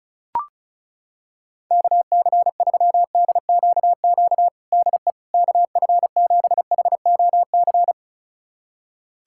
Morse code, heard as a keyed tone, about 36 wpm, 700 Hz. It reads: KC3DQQ DE KF7HOC
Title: USA Callsign DE Callsign 35wpm